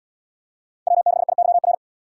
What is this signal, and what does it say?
Morse code, heard as a keyed tone, about 76 wpm, 700 Hz. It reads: W6IYR